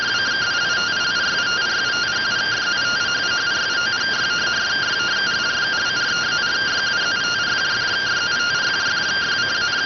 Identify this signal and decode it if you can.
Two-tone teleprinter signal, about 170 Hz shift.